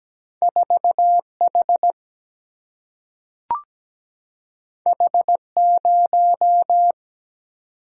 Morse code, heard as a keyed tone, about 17 wpm, 700 Hz. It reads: 4H H0
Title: Single-Letter Single-Number 17wpm